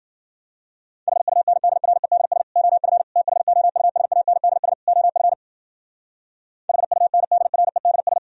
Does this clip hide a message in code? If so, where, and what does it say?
Yes: Morse code from 1.1 to 8.2 s, heard as a keyed tone, about 56 wpm, 700 Hz. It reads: HUNDREDS OF THOUSANDS OF HUNDREDS